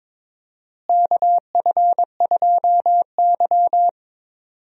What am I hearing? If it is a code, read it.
Morse code, heard as a keyed tone, about 22 wpm, 700 Hz. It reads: KF2Y
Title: USA Call Signs - 4 Characters - Repeated 2x - 22wpm